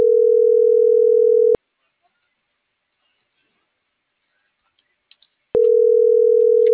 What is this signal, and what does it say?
Ringback tone (North America), 440 + 480 Hz.